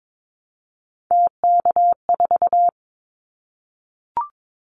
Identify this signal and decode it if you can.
Morse code, heard as a keyed tone, about 22 wpm, 700 Hz. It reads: TK4
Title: Callsign Prefixes for Top 100 DXCC - 22wpm